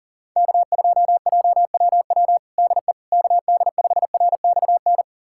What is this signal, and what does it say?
Morse code, heard as a keyed tone, about 37 wpm, 700 Hz. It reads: K2JWW DE KD5RXN
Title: USA Callsign DE Callsign 40wpm